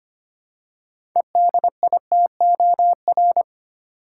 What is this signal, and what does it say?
Morse code, heard as a keyed tone, about 25 wpm, 700 Hz. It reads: EDITOR
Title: Six Letter Words - 25wpm